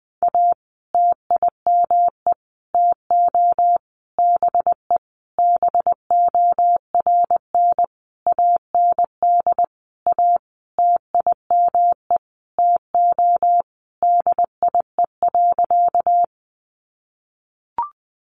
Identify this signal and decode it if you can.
Morse code, heard as a keyed tone, about 19 wpm, 700 Hz. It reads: A TIME TO BE BORN AND A TIME TO DIE.
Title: Sentences from Top 1300 Words 20wpm